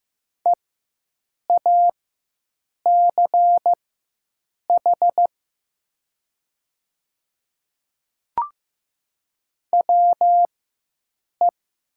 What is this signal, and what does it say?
Morse code, heard as a keyed tone, about 15 wpm, spaced as for 7 wpm, 700 Hz. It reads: EACH WE
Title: 4x Character Spacing - Top 100 Words - 15wpm